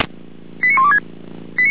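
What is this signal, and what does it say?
Five-tone selective call (ZVEI), heard as quick runs of five tones, tones of about 70 ms. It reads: 89126
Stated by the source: ZVEI five-tone selective call 89025 received via SDR.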